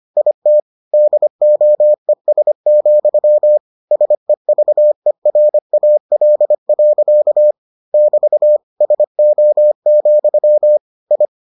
Morse code, heard as a keyed tone, about 25 wpm, 600 Hz. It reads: IT DOES, SEVERAL. <BT> SO, I